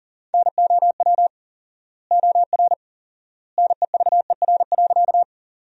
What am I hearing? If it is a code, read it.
Morse code, heard as a keyed tone, about 40 wpm, 700 Hz. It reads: NOW OR NEVER.
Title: 4x Word-Spacing - Sentences from Top 300 Words - 40wpm